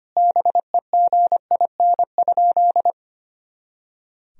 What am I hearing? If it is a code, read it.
Morse code, heard as a keyed tone, about 25 wpm, 700 Hz. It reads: BEGIN?
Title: Sentences from Top 300 Words and Abbreviations 25wpm